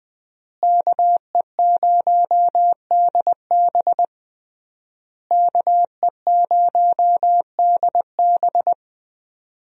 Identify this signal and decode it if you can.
Morse code, heard as a keyed tone, about 20 wpm, 700 Hz. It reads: KE0DB KE0DB
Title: USA Call Signs - 5 Characters - Repeated 3x - 20wpm